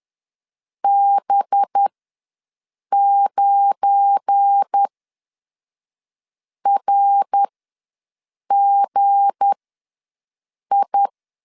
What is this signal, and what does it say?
Morse code, heard as a keyed tone, about 10 wpm, spaced as for 5 wpm, 800 Hz. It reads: B9RGI
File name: LEZIONI / 8 YJ45 / 8 YJ ws 10wpm 4043